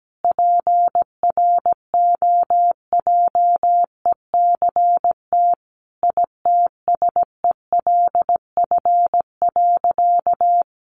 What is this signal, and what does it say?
Morse code, heard as a keyed tone, about 17 wpm, 700 Hz. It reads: PROJECT ITSELF.
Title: Sentences from Top 1400 Words 17wpm